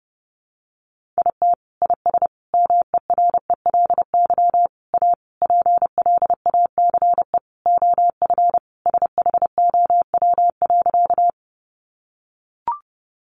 Morse code, heard as a keyed tone, about 30 wpm, 700 Hz. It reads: IT IS MERELY A PLACE OF SHOW.
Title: Sentences from Top 1400 Words 30wpm